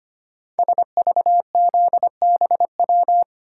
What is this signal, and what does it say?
Morse code, heard as a keyed tone, about 25 wpm, 700 Hz. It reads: SVZBW
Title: Code Groups - 5 Letters - 25wpm